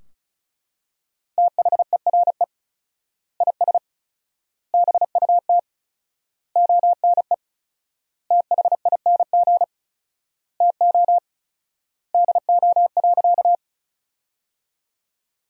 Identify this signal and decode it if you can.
Morse code, heard as a keyed tone, about 34 wpm, 700 Hz. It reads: THERE IS BUT ONE THING TO DO.
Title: 4x Word-Spacing - Sentences from Top 100 Words - 35wpm